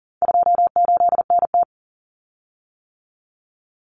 Morse code, heard as a keyed tone, about 40 wpm, 700 Hz. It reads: 28NT
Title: Common Cut Numbers - 4-Digit 40wpm